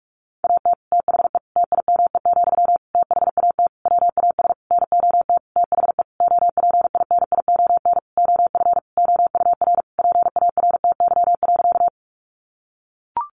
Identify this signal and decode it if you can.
Morse code, heard as a keyed tone, about 44 wpm, 700 Hz. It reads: AT THE TIME, THAT WAS NOT THE OPINION OF OUR PARTY.
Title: Sentences from Top 1400 Words 45wpm